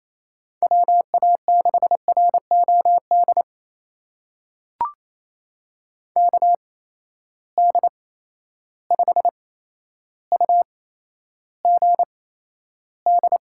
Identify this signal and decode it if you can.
Morse code, heard as a keyed tone, about 28 wpm, spaced as for 12 wpm, 700 Hz. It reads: WA6ROD KD5UGD
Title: USA Call Signs - 6 Characters - 8x Character Spacing - 28wpm